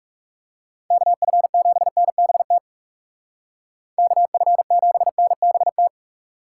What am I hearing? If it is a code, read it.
Morse code, heard as a keyed tone, about 39 wpm, 700 Hz. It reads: KF7NBT KF7NBT